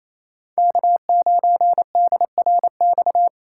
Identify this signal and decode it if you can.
Morse code, heard as a keyed tone, about 28 wpm, 700 Hz. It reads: K9DRX
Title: USA Call Signs - 5 Characters - 4x Character Spacing - 28wpm